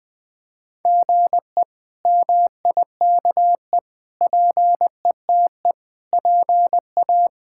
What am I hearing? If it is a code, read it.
Morse code, heard as a keyed tone, about 20 wpm, 700 Hz. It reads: GE MIKE PETE PA
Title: K1USN SST QSO Practice - 20wpm